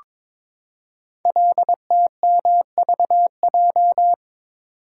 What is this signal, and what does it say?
Morse code, heard as a keyed tone, about 22 wpm, 700 Hz. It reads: LTMVJ
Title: Code Groups - 5 Letters - 22wpm